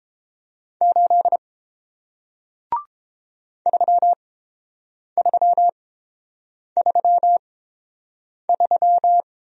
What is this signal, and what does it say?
Morse code, heard as a keyed tone, about 30 wpm, spaced as for 7 wpm, 700 Hz. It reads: 8 3333